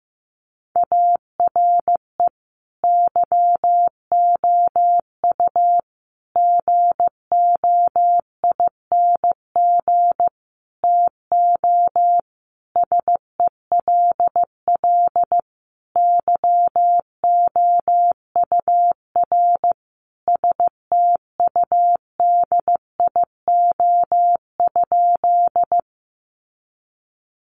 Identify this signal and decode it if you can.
Morse code, heard as a keyed tone, about 15 wpm, 700 Hz. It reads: ARE YOU GOING TO SELL YOUR STUDIO?
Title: Sentences from Top 1200 Words 15wpm